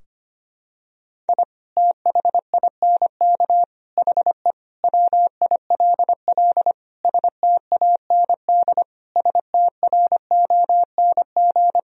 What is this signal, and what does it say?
Morse code, heard as a keyed tone, about 25 wpm, 700 Hz. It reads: I THINK HE WILL STAND STRONG
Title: Sentences from Top 1000 Words - Review Part 3 - 25wpm